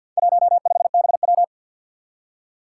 Morse code, heard as a keyed tone, about 51 wpm, 700 Hz. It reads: 15BW
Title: Four Challenging Letter-Number Combinations 50wpm